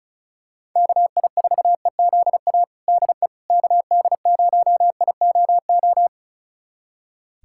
Morse code, heard as a keyed tone, about 36 wpm, 700 Hz. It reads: KI4EZA DE KD0IOO
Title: USA Callsign DE Callsign - 6 Characters - 35wpm